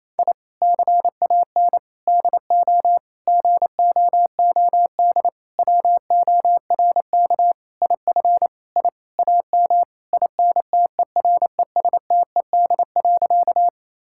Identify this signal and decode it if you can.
Morse code, heard as a keyed tone, about 28 wpm, 700 Hz. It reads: I CAN DO GOOD WORK IF I AM INTERESTED.
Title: Sentences from Top 1000 Words - Review Part 6 - 28wpm